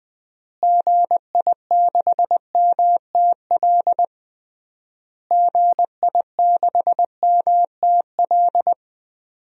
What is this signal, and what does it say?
Morse code, heard as a keyed tone, about 21 wpm, 700 Hz. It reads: GI6MTL GI6MTL
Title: DXCC-100 Call Signs - 6 Characters - Repeated 3x - 20wpm